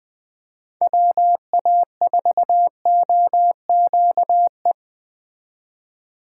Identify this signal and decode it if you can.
Morse code, heard as a keyed tone, about 20 wpm, 700 Hz. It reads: WA4OQE
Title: Speed Racing - USA Call Signs 17wpm